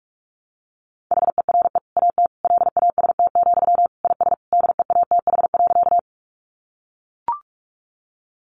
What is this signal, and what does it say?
Morse code, heard as a keyed tone, about 44 wpm, 700 Hz. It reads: HERE AT LAST, IS DEATH.
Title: Sentences from Top 1100 Words 45wpm